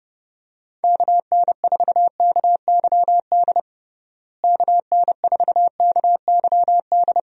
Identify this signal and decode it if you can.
Morse code, heard as a keyed tone, about 30 wpm, 700 Hz. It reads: KN4KYD KN4KYD
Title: USA Call Signs - 6 Characters - Repeated 2x - 30wpm